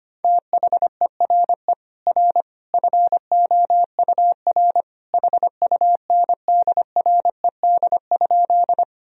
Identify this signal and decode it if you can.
Morse code, heard as a keyed tone, about 25 wpm, 700 Hz. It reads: THERE R FOUR HUNDRED?
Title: Sentences from Top 400 Words and Abbreviations 25wpm